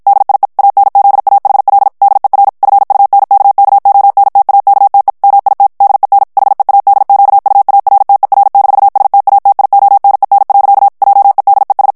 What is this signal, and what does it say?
Morse code, heard as a keyed tone, about 50 wpm, 800 Hz. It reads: DIE ANZAHL DER FUNKKONTAKTE MIT DEN HEADQUARTER-STATIONEN. JEDES
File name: morse mp3 / ALLEMAND texte clair 2 vitesse 50 mots minute